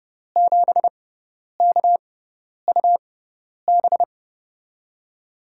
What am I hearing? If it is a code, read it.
Morse code, heard as a keyed tone, about 32 wpm, spaced as for 10 wpm, 700 Hz. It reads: 7KUB
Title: USA Call Signs - 6x Character Spacing - 30wpm